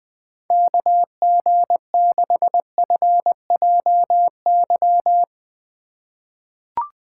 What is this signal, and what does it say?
Morse code, heard as a keyed tone, about 20 wpm, 700 Hz. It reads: KG6FJY
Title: USA Call Signs - 6 Characters - 8x Character Spacing - 20wpm